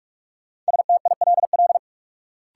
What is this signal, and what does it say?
Morse code, heard as a keyed tone, about 46 wpm, 700 Hz. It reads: STILL
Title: Sentences from Top 300 Words - New Word Drill - 45wpm